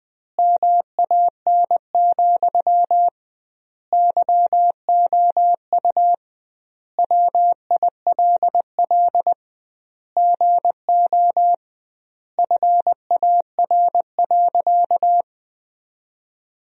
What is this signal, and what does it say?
Morse code, heard as a keyed tone, about 21 wpm, 700 Hz. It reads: MAN, YOU WILL GO FAR.
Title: 2x Word-Spacing - Sentences from Top 400 Words - 20wpm